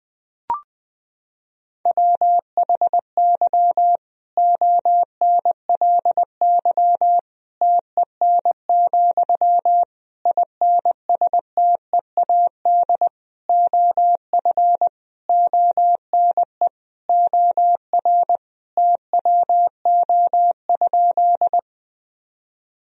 Morse code, heard as a keyed tone, about 20 wpm, 700 Hz. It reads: WHY ONLY TEN, INSTEAD OF ONE OR TWO?
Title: Sentences from Top 1000 Words - Review Part 4 - 20wpm